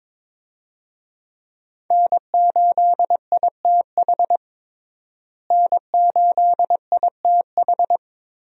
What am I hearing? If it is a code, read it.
Morse code, heard as a keyed tone, about 22 wpm, 700 Hz. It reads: N8ITH N8ITH
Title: USA Call Signs - 5 Characters - Repeated 3x - 22wpm